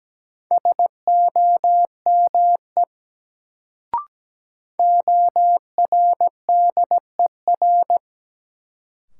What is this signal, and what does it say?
Morse code, heard as a keyed tone, about 17 wpm, 700 Hz. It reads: SOME ORDER
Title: Sets of 2 Words - New Word Drill - 17wpm